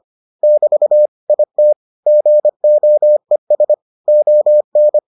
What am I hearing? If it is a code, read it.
Morse code, heard as a keyed tone, about 25 wpm, 600 Hz. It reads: <BT> IT GOES ON